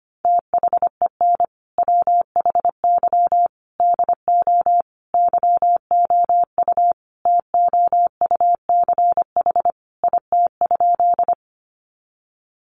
Morse code, heard as a keyed tone, about 25 wpm, 700 Hz. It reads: THEN WHY DO YOU TOUCH IT?
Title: Sentences from Top 1200 Words 25wpm